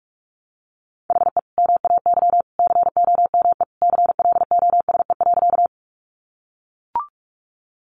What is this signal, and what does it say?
Morse code, heard as a keyed tone, about 44 wpm, 700 Hz. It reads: HE MAY COME CLOSE.
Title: Sentences from Top 300 Words 45wpm